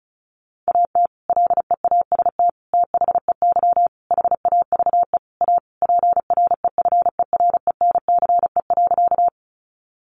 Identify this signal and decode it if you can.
Morse code, heard as a keyed tone, about 34 wpm, 700 Hz. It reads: AT LEAST THEY HAVE A PREFERENCE.
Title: Sentences from Top 1500 Words 35wpm